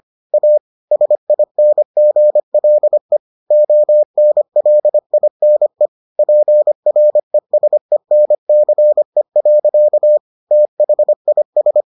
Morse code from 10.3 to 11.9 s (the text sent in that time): THIS